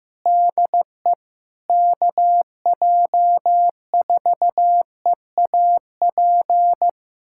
Morse code, heard as a keyed tone, about 15 wpm, 700 Hz. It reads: DE KJ4EAP
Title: USA Callsign DE Callsign 15wpm